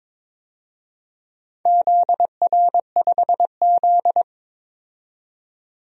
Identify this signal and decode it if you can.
Morse code, heard as a keyed tone, about 22 wpm, 700 Hz. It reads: ZR5Z
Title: DXCC-100 Call Signs - 4 Characters - 22wpm